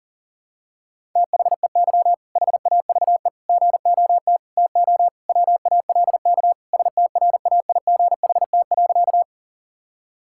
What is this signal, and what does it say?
Morse code, heard as a keyed tone, about 37 wpm, 700 Hz. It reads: THEY HAVE GOT TO WALK STRAIGHT.
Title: Sentences from Top 1000 Words - Review Part 6 - 40wpm